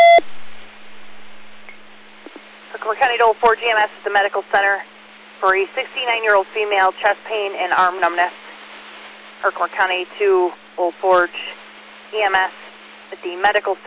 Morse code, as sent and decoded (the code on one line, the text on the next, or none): none